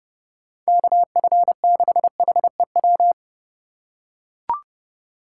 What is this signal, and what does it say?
Morse code, heard as a keyed tone, about 30 wpm, 700 Hz. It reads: KF6HEW
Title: USA Call Signs - 6 Characters - 2x Character Spacing - 30wpm